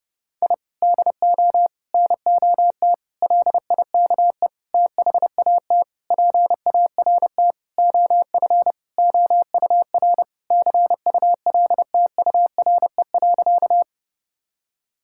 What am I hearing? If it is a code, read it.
Morse code, heard as a keyed tone, about 30 wpm, 700 Hz. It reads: I DO NOT LIKE THAT PART OF OUR CULTURE.
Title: Sentences from Top 1000 Words - Review Part 6 - 30wpm